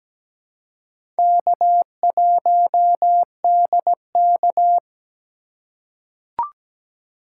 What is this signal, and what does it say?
Morse code, heard as a keyed tone, about 17 wpm, 700 Hz. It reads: K1DK
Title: USA Call Signs - 4 Characters - 4x Character Spacing - 17wpm